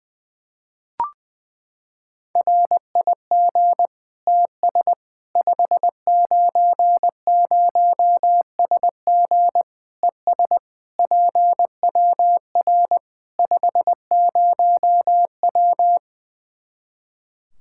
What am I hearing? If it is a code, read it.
Morse code, heard as a keyed tone, about 20 wpm, 700 Hz. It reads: RIG TS 590SG ES PWR 50W
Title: Practice QSO - Exchange 2 - Info - 20wpm